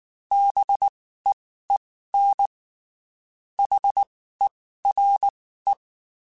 Morse code, heard as a keyed tone, about 19 wpm, spaced as for 14 wpm, 800 Hz. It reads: BEEN HERE
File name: (spaced) / QSOs Spaced-mp3s / WD0BGP 1 @18 sp